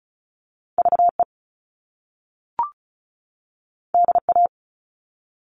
Morse code, heard as a keyed tone, about 36 wpm, 700 Hz. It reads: VE DA
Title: All Cut Numbers - 2-Digit 35wpm